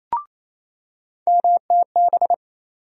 Morse code, heard as a keyed tone, about 28 wpm, 700 Hz. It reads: MTB
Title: USA Call Sign Postfixes - 3 Letters 28wpm